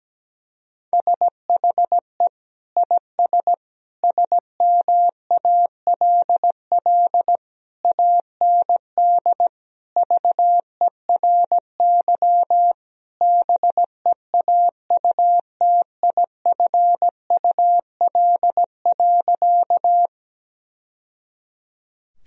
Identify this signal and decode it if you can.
Morse code, heard as a keyed tone, about 17 wpm, 700 Hz. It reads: SHE IS SMALL AND VERY BEAUTIFUL.
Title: Sentences from Top 1000 Words - Review Part 1 - 17wpm